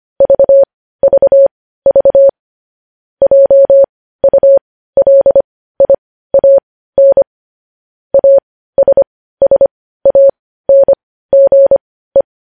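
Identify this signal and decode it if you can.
Morse code, heard as a keyed tone, about 25 wpm, spaced as for 15 wpm, 550 Hz. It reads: VVV JULIAN ASSANGE
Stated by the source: Morse code news headlines.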